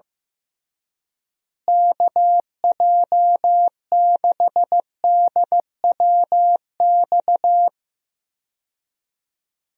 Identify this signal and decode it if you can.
Morse code, heard as a keyed tone, about 15 wpm, 700 Hz. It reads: KJ6DWX
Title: USA Call Signs - 6 Characters - Repeated 3x - 15wpm